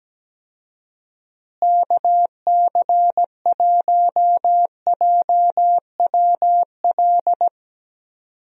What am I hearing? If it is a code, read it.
Morse code, heard as a keyed tone, about 17 wpm, 700 Hz. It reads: KC1JWL
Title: USA Call Signs - 6 Characters - Repeated 2x - 17wpm